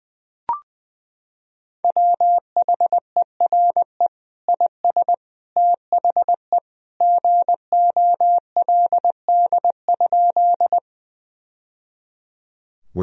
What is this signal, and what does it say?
Morse code, heard as a keyed tone, about 21 wpm, 700 Hz. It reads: WHERE IS THE GOLD?
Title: Sentences from Top 1000 Words - Review Part 3 - 20wpm